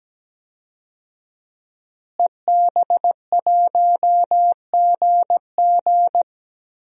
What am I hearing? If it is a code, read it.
Morse code, heard as a keyed tone, about 17 wpm, 700 Hz. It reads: EB1GG
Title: DXCC-100 Call Signs - 5 Characters - Repeated 2x - 17wpm